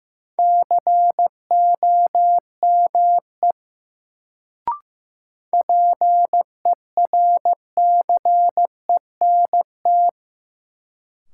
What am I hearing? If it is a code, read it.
Morse code, heard as a keyed tone, about 15 wpm, 700 Hz. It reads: COME PERCENT
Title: Sets of 2 Words - New Word Drill - 15wpm